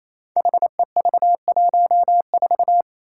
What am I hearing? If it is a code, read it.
Morse code, heard as a keyed tone, about 28 wpm, 700 Hz. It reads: HEV14
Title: Five Challenging Letter-Number Combinations 28wpm